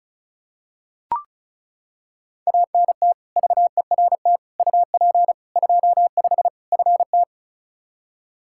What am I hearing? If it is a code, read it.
Morse code, heard as a keyed tone, about 36 wpm, 700 Hz. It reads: ANT VERT UP 25 FT
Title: K1USN SST QSO Practice 35wpm